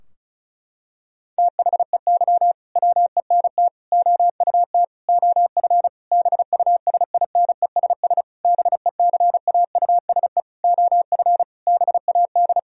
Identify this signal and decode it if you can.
Morse code, heard as a keyed tone, about 34 wpm, 700 Hz. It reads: THEY WENT OUT OF BUSINESS BECAUSE OF BAD
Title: Sentences from Top 1000 Words - Review Part 4 - 35wpm